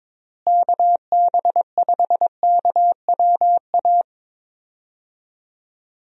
Morse code, heard as a keyed tone, about 22 wpm, 700 Hz. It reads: KB5KWA
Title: USA Call Signs - 6 Characters - Repeated 3x - 22wpm